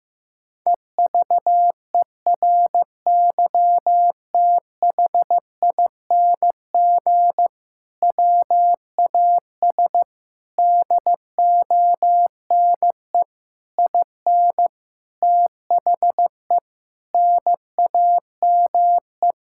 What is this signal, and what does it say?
Morse code, heard as a keyed tone, about 15 wpm, 700 Hz. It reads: EVERYTHING WAS DONE IN THE NAME
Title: Sentences from Top 1600 Words 15wpm